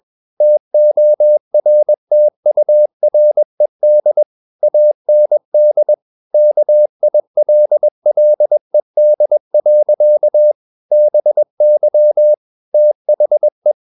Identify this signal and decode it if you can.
Morse code, heard as a keyed tone, about 21 wpm, 600 Hz. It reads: TORTURED AND KILLED. BY THE
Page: CW med Gnister